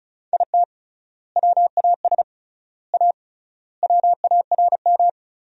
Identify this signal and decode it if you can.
Morse code, heard as a keyed tone, about 34 wpm, 700 Hz. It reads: IT WAS A WARM
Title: 3x Word-Spacing - Sentences from Top 500 Words - 35wpm